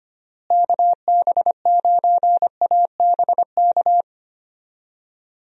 Morse code, heard as a keyed tone, about 25 wpm, 700 Hz. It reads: KB9ABK
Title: USA Call Signs - 6 Characters - Repeated 3x - 25wpm